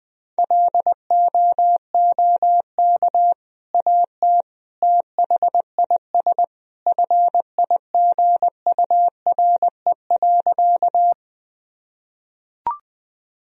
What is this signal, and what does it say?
Morse code, heard as a keyed tone, about 21 wpm, 700 Hz. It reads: LOOK AT THIS FIGURE.
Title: Sentences from Top 1000 Words - Review Part 2 - 20wpm